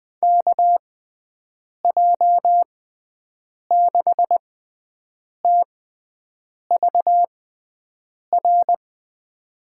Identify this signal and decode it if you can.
Morse code, heard as a keyed tone, about 21 wpm, spaced as for 7 wpm, 700 Hz. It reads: KJ6TVR
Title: USA Call Signs - 6 Characters - 6x Character Spacing - 20wpm